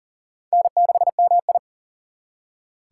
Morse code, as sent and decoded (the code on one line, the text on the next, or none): -. -.... -- ..
N6MI